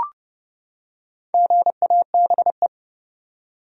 Morse code, heard as a keyed tone, about 30 wpm, 700 Hz. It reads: GABE
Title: USA Common Names - Max 5 Letters - 30wpm